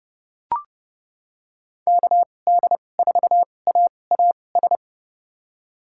Morse code, heard as a keyed tone, about 30 wpm, spaced as for 22 wpm, 700 Hz. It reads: KD4AAS
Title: USA Call Signs - 6 Characters - 2x Character Spacing - 30wpm